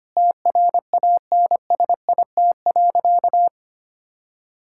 Morse code, heard as a keyed tone, about 25 wpm, 700 Hz. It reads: TRANSIT.
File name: practice / 25wpm / Difficult Sentences with Hints - 25wpm